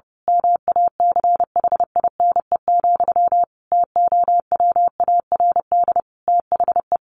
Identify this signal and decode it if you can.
Morse code, heard as a keyed tone, about 30 wpm, 700 Hz. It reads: MACHINE, TOWARD THE
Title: A Sound Of Thunder 30wpm